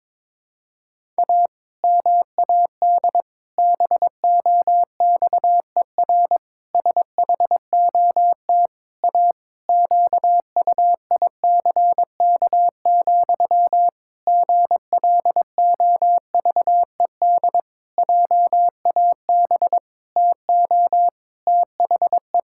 Morse code, heard as a keyed tone, about 22 wpm, 700 Hz. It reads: A MAD BOXER SHOT A QUICK, GLOVED JAB TO THE
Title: Pangram Challenge v2 - 22wpm